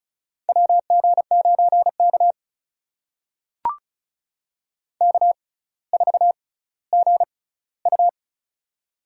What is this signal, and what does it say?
Morse code, heard as a keyed tone, about 36 wpm, spaced as for 18 wpm, 700 Hz. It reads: WG9K K4GU
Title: USA Call Signs - 4 Characters - 6x Character Spacing - 35wpm